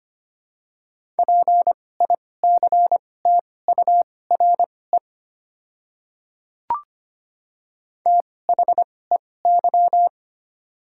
Morse code, heard as a keyed tone, about 25 wpm, spaced as for 18 wpm, 690 Hz. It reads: PICTURE THEY
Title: 2x Character Spacing - Top 200 Words - 25wpm